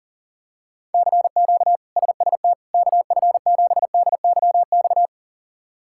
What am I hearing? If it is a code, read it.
Morse code, heard as a keyed tone, about 37 wpm, 700 Hz. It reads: CQ SST KF7DYX
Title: K1USN SST QSO Practice - 40wpm